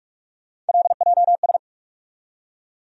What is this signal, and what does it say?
Morse code, heard as a keyed tone, about 43 wpm, 700 Hz. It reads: LJS
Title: USA Call Sign Postfixes - 3 Letters 45wpm